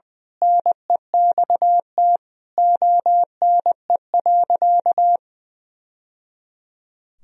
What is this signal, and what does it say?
Morse code, heard as a keyed tone, about 20 wpm, 700 Hz. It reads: NEXT ONE.
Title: Sentences from Top 1000 Words - Review Part 1 - 20wpm